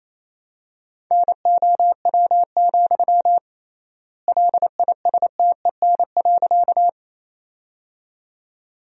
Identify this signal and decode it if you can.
Morse code, heard as a keyed tone, about 28 wpm, 700 Hz. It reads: NOW, LISTEN.
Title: 3x Word-Spacing - Sentences from Top 400 Words - 28wpm